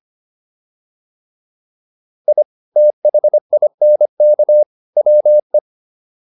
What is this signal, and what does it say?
Morse code, heard as a keyed tone, about 25 wpm, 600 Hz. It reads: I THINK WE
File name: WD205 25